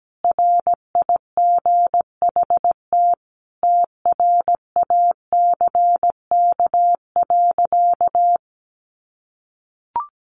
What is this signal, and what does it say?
Morse code, heard as a keyed tone, about 17 wpm, 700 Hz. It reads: RIGHT TRACK.
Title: Sentences from Top 1100 Words 17wpm